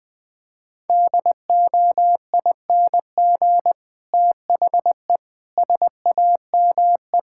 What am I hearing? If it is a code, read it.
Morse code, heard as a keyed tone, about 20 wpm, 700 Hz. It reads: DOING THE SAME
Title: Sentences with 8-Word Phrases - Spoken Beginning - 20wpm